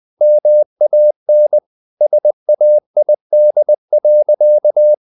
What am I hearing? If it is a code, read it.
Morse code, heard as a keyed tone, about 20 wpm, 600 Hz. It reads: MAN SAID.
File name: the old man and the sea part 2 20wpm